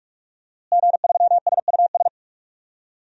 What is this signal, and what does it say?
Morse code, heard as a keyed tone, about 46 wpm, 700 Hz. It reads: G3SUS
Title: DXCC-100 Call Signs - 5 Characters - Repeated 2x - 45wpm